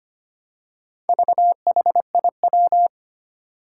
Morse code, heard as a keyed tone, about 25 wpm, 700 Hz. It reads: VHIW
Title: Four Challenging Letter-Number Combinations 25wpm